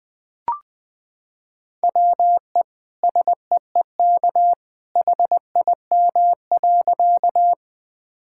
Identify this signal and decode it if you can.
Morse code, heard as a keyed tone, about 19 wpm, 700 Hz. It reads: WE SEEK HIM.
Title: Sentences from Top 1600 Words 20wpm